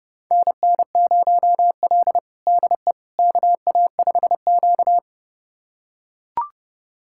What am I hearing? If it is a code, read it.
Morse code, heard as a keyed tone, about 30 wpm, 700 Hz. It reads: NN0L DE KA5Q
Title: USA Callsign DE Callsign - 4 Characters - 30wpm